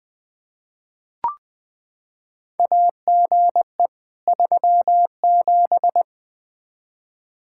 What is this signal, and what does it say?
Morse code, heard as a keyed tone, about 20 wpm, 700 Hz. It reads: AGE 37
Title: Practice QSO - Exchange 3 - More Info - 20wpm